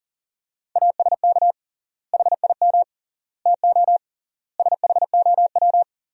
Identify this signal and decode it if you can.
Morse code, heard as a keyed tone, about 40 wpm, 700 Hz. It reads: ASK HIM TO SHOW
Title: 3x Word-Spacing - Sentences from Top 200 Words - 40wpm